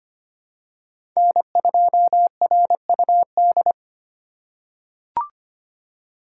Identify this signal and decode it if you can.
Morse code, heard as a keyed tone, about 25 wpm, 700 Hz. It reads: N2RUD
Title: USA Call Signs - 4x Character Spacing - 25wpm